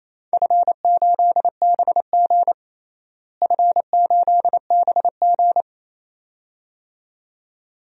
Morse code, heard as a keyed tone, about 28 wpm, 700 Hz. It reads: F8BG F8BG
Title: DXCC-100 Call Signs - 4 Characters - Repeated 2x - 28wpm